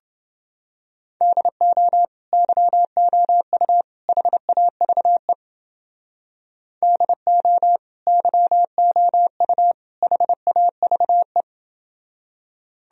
Morse code, heard as a keyed tone, about 30 wpm, 700 Hz. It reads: DO YOU HAVE DO YOU HAVE